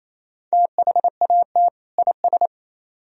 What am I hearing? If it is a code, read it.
Morse code, heard as a keyed tone, about 35 wpm, 700 Hz. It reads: THAT IS